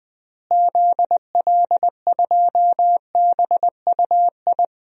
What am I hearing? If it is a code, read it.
Morse code, heard as a keyed tone, about 20 wpm, 700 Hz. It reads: ZL2BUI
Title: DXCC-100 Call Signs - 6 Characters - Repeated 2x - 20wpm